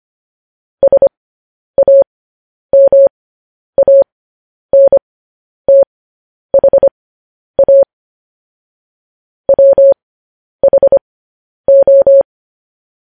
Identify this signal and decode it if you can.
Morse code, heard as a keyed tone, about 25 wpm, spaced as for 10 wpm, 550 Hz. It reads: SAMANTHA WHO